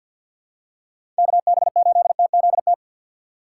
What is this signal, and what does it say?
Morse code, heard as a keyed tone, about 50 wpm, 700 Hz. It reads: KB8TZT
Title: USA Call Signs - 3x Character Spacing - 50wpm